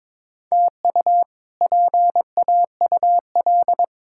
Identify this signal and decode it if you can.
Morse code, heard as a keyed tone, about 22 wpm, 700 Hz. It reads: TU PAUL
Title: K1USN SST QSO Practice - 22wpm